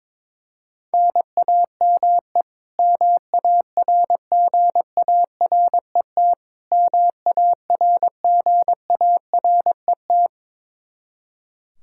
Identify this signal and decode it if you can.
Morse code, heard as a keyed tone, about 22 wpm, 700 Hz. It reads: NAME MARGARET MARGARET
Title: K1USN SST QSO Practice 22wpm